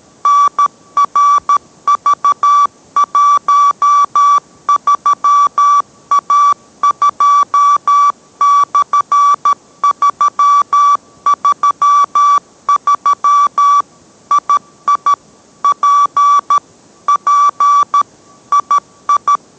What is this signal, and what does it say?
Morse code, heard as a keyed tone, about 13 wpm, 1150 Hz. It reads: NRV13A2/333 II P P II